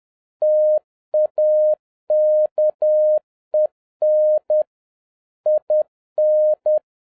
Morse code, heard as a keyed tone, about 10 wpm, 610 Hz. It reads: TAKEN IN
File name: 2011-07-mp3 / n31 THE 3.11 2 10wpm